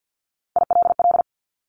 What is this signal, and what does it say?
Morse code, heard as a keyed tone, about 50 wpm, 700 Hz. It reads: ILL
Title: Top 1751-2000 Words 50wpm